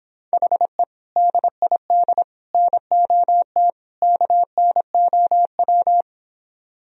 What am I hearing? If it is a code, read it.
Morse code, heard as a keyed tone, about 26 wpm, 700 Hz. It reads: HE DID NOT KNOW